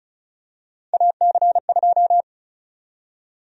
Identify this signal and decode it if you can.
Morse code, heard as a keyed tone, about 36 wpm, 700 Hz. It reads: AC2
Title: USA Call Sign Prefixes - Repeated 2x - 35wpm